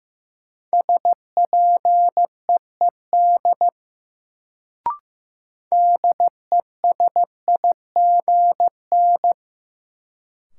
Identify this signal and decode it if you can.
Morse code, heard as a keyed tone, about 15 wpm, 700 Hz. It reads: SPEED DESIGN
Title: Sentences from Top 800 Words - New Word Drill - 15wpm